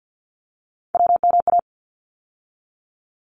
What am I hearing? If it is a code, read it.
Morse code, heard as a keyed tone, about 50 wpm, 700 Hz. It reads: WMA
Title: ARRL Sections 50wpm